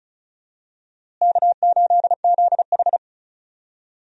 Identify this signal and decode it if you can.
Morse code, heard as a keyed tone, about 36 wpm, 700 Hz. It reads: K8ZH
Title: USA Call Signs - 4 Characters - Repeated 3x - 35wpm